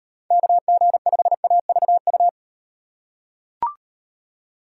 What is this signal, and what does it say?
Morse code, heard as a keyed tone, about 33 wpm, 700 Hz. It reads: KG5AVU